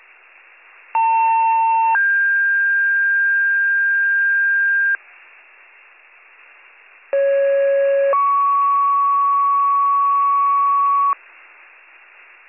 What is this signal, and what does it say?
Two-tone page, heard as a one-second tone, then a three-second tone, about 900 then 1600 Hz, about 550 then 1120 Hz. It is heard from 0.9 to 11.1 s.